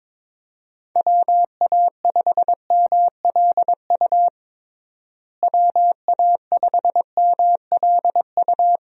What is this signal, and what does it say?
Morse code, heard as a keyed tone, about 21 wpm, 700 Hz. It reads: WA5MLU WA5MLU
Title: USA Call Signs - 6 Characters - Repeated 3x - 22wpm